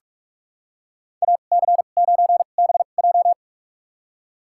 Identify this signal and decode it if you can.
Morse code, heard as a keyed tone, about 46 wpm, spaced as for 33 wpm, 700 Hz. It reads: AC9BJ
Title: USA Call Signs - 5 Characters - 2x Character Spacing - 45wpm